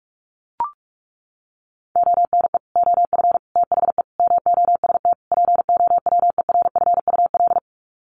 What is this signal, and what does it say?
Morse code, heard as a keyed tone, about 44 wpm, 700 Hz. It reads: ONE OF THE MOST POWERFUL
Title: Sets of 5 Words Encore 45wpm